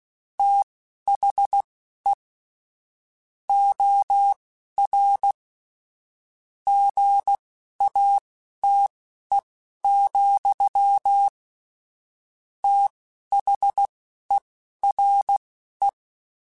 Morse code, heard as a keyed tone, about 16 wpm, spaced as for 12 wpm, 800 Hz. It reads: THE OR GATE, THERE
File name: Technical Subjects / Technical Subjects-mp3s / OR Gate @10